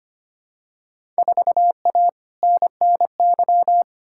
Morse code, heard as a keyed tone, about 25 wpm, 700 Hz. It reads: 4A NNY
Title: Field Day Exchanges - 25wpm